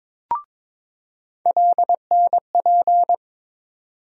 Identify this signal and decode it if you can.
Morse code, heard as a keyed tone, about 22 wpm, 700 Hz. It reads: LNP
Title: USA Call Sign Postfixes - 3 Letters 22wpm